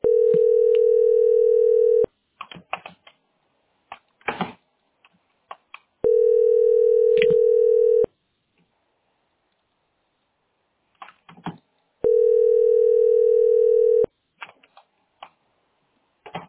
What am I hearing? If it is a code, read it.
Ringback tone (North America), 440 + 480 Hz.